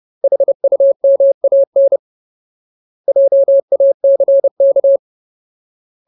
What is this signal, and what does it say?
Morse code, heard as a keyed tone, about 30 wpm, 550 Hz. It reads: HUMAN JACK
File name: Ultimate Word List / 30WPM Ultimate